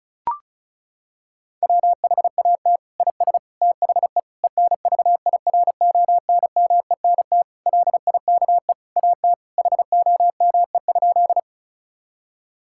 Morse code, heard as a keyed tone, about 34 wpm, 700 Hz. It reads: WHAT IS THE ENVIRONMENT LIKE AT HOME?
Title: Sentences from Top 1000 Words - Review Part 5 - 35wpm